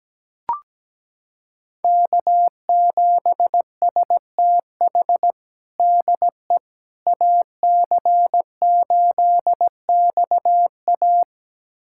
Morse code, heard as a keyed tone, about 17 wpm, 700 Hz. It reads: K7STH DE AC8XA
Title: USA Callsign DE Callsign - 5 Characters - 17wpm